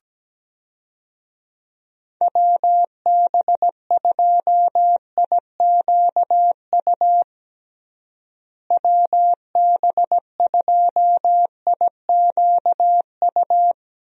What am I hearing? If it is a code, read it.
Morse code, heard as a keyed tone, about 17 wpm, 700 Hz. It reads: WB2IQU WB2IQU
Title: USA Call Signs - 6 Characters - Repeated 3x - 17wpm